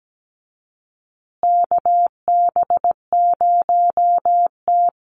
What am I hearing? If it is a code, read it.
Morse code, heard as a keyed tone, about 17 wpm, 700 Hz. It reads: KB0T
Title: USA Call Signs Repeated 3x 17wpm